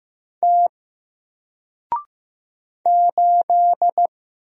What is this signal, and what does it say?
Morse code, heard as a keyed tone, about 15 wpm, 700 Hz. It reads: T 8
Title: Single Character (Letter:Number:Punctuation:Prosign) 15wpm